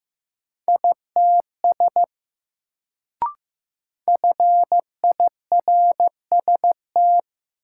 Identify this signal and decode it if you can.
Morse code, heard as a keyed tone, about 15 wpm, 700 Hz. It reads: ITS FIRST
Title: Sets of 2 Words - New Word Drill - 15wpm